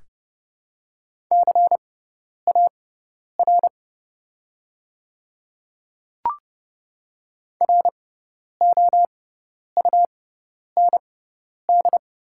Morse code, heard as a keyed tone, about 29 wpm, spaced as for 10 wpm, 700 Hz. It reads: CAR ROUND
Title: 6x Character Spacing - Top 300 Words - 30wpm